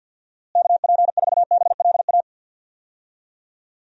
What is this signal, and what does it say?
Morse code, heard as a keyed tone, about 52 wpm, 700 Hz. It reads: KP4BLA
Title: DXCC-100 Call Signs - 6 Characters - Repeated 3x - 50wpm